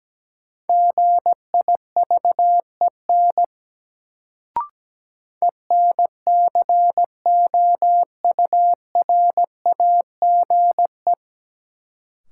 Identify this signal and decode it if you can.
Morse code, heard as a keyed tone, about 17 wpm, 700 Hz. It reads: GIVEN ENCOURAGE
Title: Sentences from Top 1400 Words - New Word Drill - 17wpm